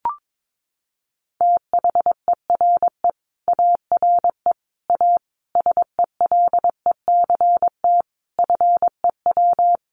Morse code, heard as a keyed tone, about 22 wpm, 700 Hz. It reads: THERE ARE A SELECT FEW
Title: Sentences from Top 1300 Words 22wpm